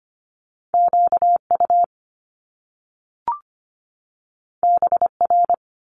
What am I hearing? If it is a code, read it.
Morse code, heard as a keyed tone, about 25 wpm, 700 Hz. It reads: QU BR
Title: Top Two Letter Combinations in Words 25wpm